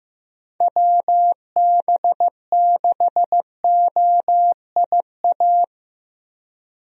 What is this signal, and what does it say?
Morse code, heard as a keyed tone, about 15 wpm, 700 Hz. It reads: WB6OIA
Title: USA Call Signs - 6 Characters - 15wpm